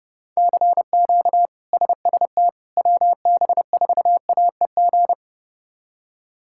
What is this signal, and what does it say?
Morse code, heard as a keyed tone, about 30 wpm, 700 Hz. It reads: CQ SST WB4AEG
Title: K1USN SST QSO Practice - 30wpm